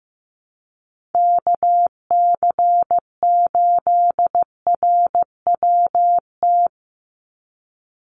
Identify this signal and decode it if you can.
Morse code, heard as a keyed tone, about 15 wpm, 700 Hz. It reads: KC8RWT
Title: USA Call Signs 15wpm